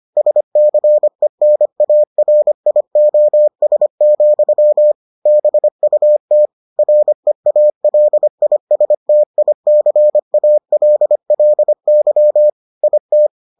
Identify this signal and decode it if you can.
Morse code, heard as a keyed tone, about 25 wpm, 600 Hz. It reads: SCENARIOS, BUT REALISTICALLY IT